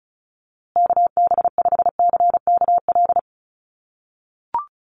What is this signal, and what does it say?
Morse code, heard as a keyed tone, about 34 wpm, 700 Hz. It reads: KB5CKL
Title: USA Call Signs Repeated 3x 35wpm